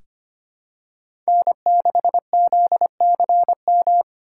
Morse code, heard as a keyed tone, about 25 wpm, 700 Hz. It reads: N6ZCM
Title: USA Call Signs - 6x Character Spacing - 25wpm